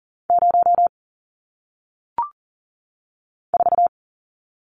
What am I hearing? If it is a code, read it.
Morse code, heard as a keyed tone, about 42 wpm, 700 Hz. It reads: 0 4
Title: Numbers - 1-Digit 40wpm